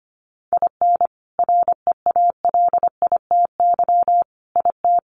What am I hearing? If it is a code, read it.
Morse code, heard as a keyed tone, about 25 wpm, 700 Hz. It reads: IN REALITY IT
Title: Sentences from Top 1200 Words 25wpm